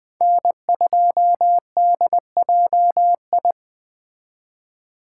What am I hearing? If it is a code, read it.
Morse code, heard as a keyed tone, about 24 wpm, 700 Hz. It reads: N2DJI
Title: Speed Racing - USA Call Signs 20wpm